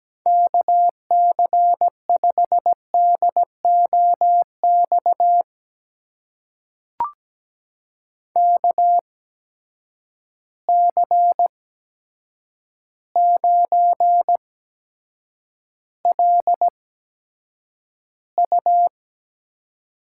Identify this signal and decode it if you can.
Morse code, heard as a keyed tone, about 17 wpm, spaced as for 8 wpm, 700 Hz. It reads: KC5DOX KC9LU
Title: USA Call Signs - 6 Characters - 8x Character Spacing - 17wpm